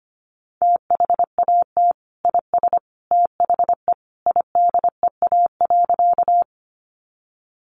Morse code, heard as a keyed tone, about 25 wpm, 700 Hz. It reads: THAT IS THE IDEA.
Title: Sentences from Top 300 Words 25wpm